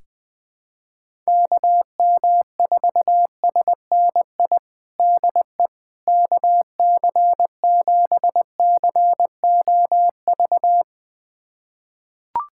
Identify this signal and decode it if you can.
Morse code, heard as a keyed tone, about 19 wpm, 700 Hz. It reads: KM4SNI DE KC7COV
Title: USA Callsign DE Callsign 20wpm